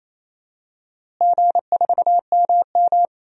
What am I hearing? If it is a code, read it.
Morse code, heard as a keyed tone, about 27 wpm, 700 Hz. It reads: G4MM
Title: DXCC-100 Call Signs - 4 Characters - Repeated 2x - 28wpm